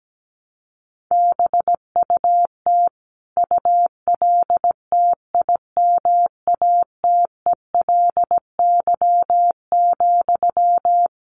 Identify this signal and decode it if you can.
Morse code, heard as a keyed tone, about 17 wpm, 700 Hz. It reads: BUT ULTIMATELY,
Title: Sentences from Top 1200 Words 17wpm